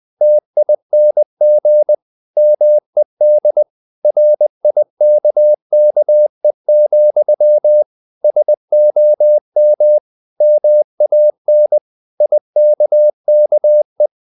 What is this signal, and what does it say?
Morse code, heard as a keyed tone, about 20 wpm, 600 Hz. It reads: TING MED RIKKE, SOM MAN IKKE
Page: Ferie 20 wpm